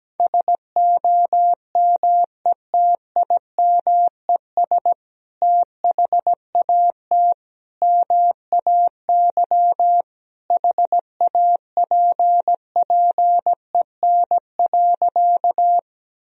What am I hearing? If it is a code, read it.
Morse code, heard as a keyed tone, about 17 wpm, 700 Hz. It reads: SOMETIMES THAT MAY HAPPEN.
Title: Sentences from Top 1000 Words - Review Part 3 - 17wpm